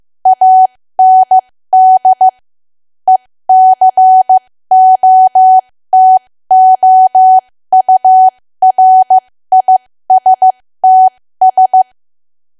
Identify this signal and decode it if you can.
Morse code, heard as a keyed tone, about 15 wpm, spaced as for 13 wpm, 750 Hz. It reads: AND ECOTOURISTS
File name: cwa / W1AW-Level2 / 13WPM